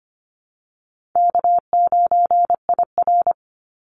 Morse code, heard as a keyed tone, about 25 wpm, 700 Hz. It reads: K9IR
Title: NA SOTA Chaser Callsigns 25wpm